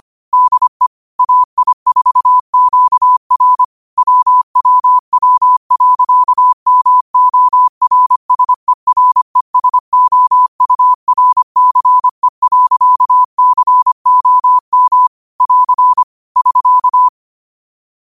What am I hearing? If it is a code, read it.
Morse code, heard as a keyed tone, about 25 wpm, 1000 Hz. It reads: DE AI4QR WWW.MORSERESOURCE.COM <AR> <SK>